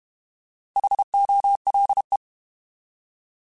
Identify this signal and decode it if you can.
Morse code, heard as a keyed tone, about 32 wpm, 800 Hz. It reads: HOLE